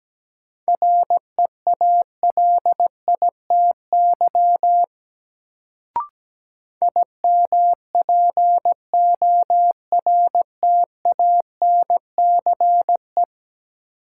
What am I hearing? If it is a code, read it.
Morse code, heard as a keyed tone, about 17 wpm, 700 Hz. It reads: REALITY IMPORTANCE
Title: Sentences from Top 1200 Words - New Word Drill - 17wpm